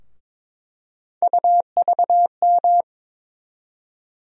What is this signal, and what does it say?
Morse code, heard as a keyed tone, about 22 wpm, 700 Hz. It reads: UVM
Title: USA Call Sign Postfixes - 3 Letters 22wpm